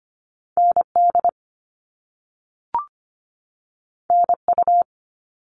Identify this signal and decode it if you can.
Morse code, heard as a keyed tone, about 25 wpm, 700 Hz. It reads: N D N U
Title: All Cut Numbers - 2-Digit 25wpm